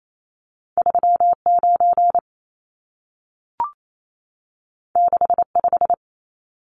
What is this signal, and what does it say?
Morse code, heard as a keyed tone, about 28 wpm, 700 Hz. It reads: 39 65
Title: Numbers - 2-Digit 28wpm